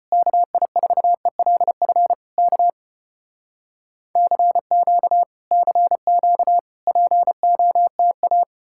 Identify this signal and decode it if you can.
Morse code, heard as a keyed tone, about 34 wpm, 700 Hz. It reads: KI4ELF K CQ CQ POTA